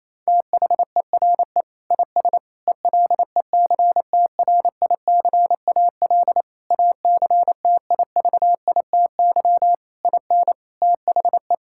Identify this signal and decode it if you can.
Morse code, heard as a keyed tone, about 28 wpm, 700 Hz. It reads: THERE IS ELECTRICAL ACTIVITY IN THE
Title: Sentences from Top 1000 Words - Review Part 6 - 28wpm